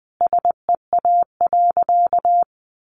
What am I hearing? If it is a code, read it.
Morse code, heard as a keyed tone, about 20 wpm, 700 Hz. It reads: SEA.
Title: Sentences from Top 1300 Words 20wpm